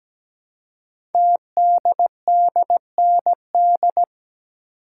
Morse code, heard as a keyed tone, about 17 wpm, 700 Hz. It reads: TDDND
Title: Code Groups - 5 Letters - 17wpm